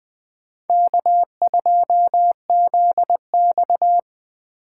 Morse code, heard as a keyed tone, about 20 wpm, 700 Hz. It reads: K2ZX
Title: Speed Racing - USA 1x2 Call Signs 17wpm